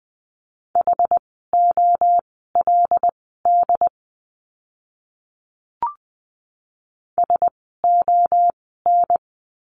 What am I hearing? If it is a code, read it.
Morse code, heard as a keyed tone, about 20 wpm, spaced as for 14 wpm, 700 Hz. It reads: HOLD SON
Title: 101 Lesson 17f - Letter u - Words 20wpm